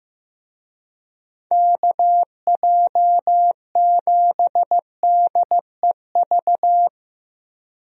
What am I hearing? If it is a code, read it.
Morse code, heard as a keyed tone, about 15 wpm, 700 Hz. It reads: KJ7DEV
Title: USA Call Signs - 6 Characters - Repeated 2x - 15wpm